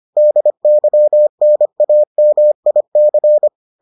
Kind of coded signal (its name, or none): Morse code